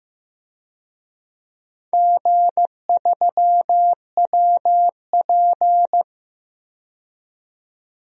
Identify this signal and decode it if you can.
Morse code, heard as a keyed tone, about 15 wpm, 700 Hz. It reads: G3WP
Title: DXCC-100 Call Signs - 4 Characters - 15wpm